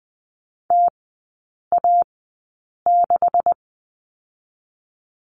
Morse code, heard as a keyed tone, about 20 wpm, spaced as for 8 wpm, 700 Hz. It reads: TA6
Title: 127 Lesson 21e - Number 6 - Three Characters 20wpm